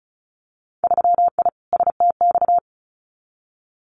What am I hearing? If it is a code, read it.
Morse code, heard as a keyed tone, about 36 wpm, 700 Hz. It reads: 3I STX
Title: Winter Field Day Exchanges 35wpm